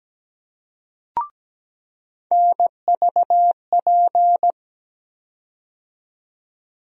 Morse code, heard as a keyed tone, about 17 wpm, 700 Hz. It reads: NVP
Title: USA Call Sign Postfixes - 3 Letters 17wpm